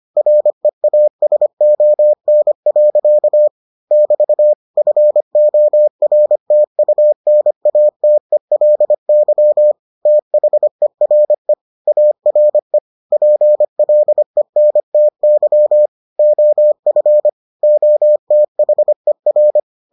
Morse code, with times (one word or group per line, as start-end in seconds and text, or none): 0.2-3.5 REASON.
3.9-4.5 <BT>
4.8-9.7 FORTUNATELY
10.1-11.5 THERE
11.9-12.8 ARE
13.1-15.9 PLENTY
16.2-17.3 OF
17.6-19.6 OTHER